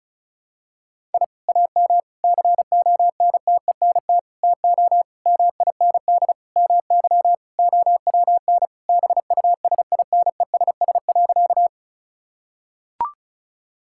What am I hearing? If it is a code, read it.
Morse code, heard as a keyed tone, about 36 wpm, 700 Hz. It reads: I AM CONTENT TO MIND MY OWN BUSINESS.
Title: Sentences from Top 1000 Words - Review Part 6 - 35wpm